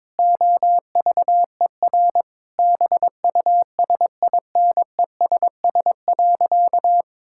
Morse code, heard as a keyed tone, about 21 wpm, 700 Hz. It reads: OVER BUSINESS.
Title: Sentences from Top 1000 Words - Review Part 5 - 22wpm